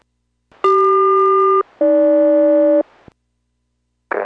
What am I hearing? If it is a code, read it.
Aircraft SELCAL, heard as two pairs of tones. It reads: CQ-AG